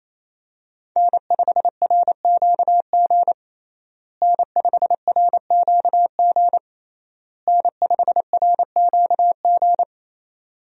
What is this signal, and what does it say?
Morse code, heard as a keyed tone, about 28 wpm, 700 Hz. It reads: N5RQG N5RQG N5RQG
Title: USA Call Signs - 5 Characters - Repeated 3x - 28wpm